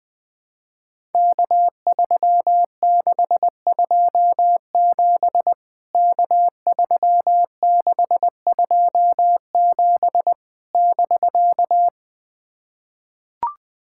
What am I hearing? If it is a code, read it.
Morse code, heard as a keyed tone, about 21 wpm, 700 Hz. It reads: K3627 K3627 <BK>
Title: POTA QSO Practice - 20wpm